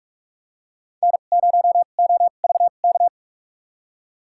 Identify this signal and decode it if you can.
Morse code, heard as a keyed tone, about 46 wpm, spaced as for 33 wpm, 700 Hz. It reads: N0OVK
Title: USA Call Signs - 2x Character Spacing - 45wpm